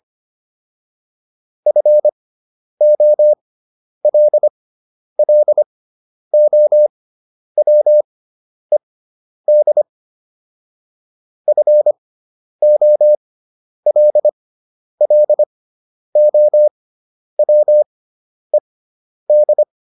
Morse code, heard as a keyed tone, about 25 wpm, spaced as for 10 wpm, 600 Hz. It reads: FOLLOWED FOLLOWED